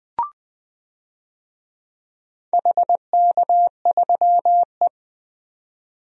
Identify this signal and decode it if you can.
Morse code, heard as a keyed tone, about 21 wpm, 700 Hz. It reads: HK3E
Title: DXCC-100 Call Signs - 4 Characters - 20wpm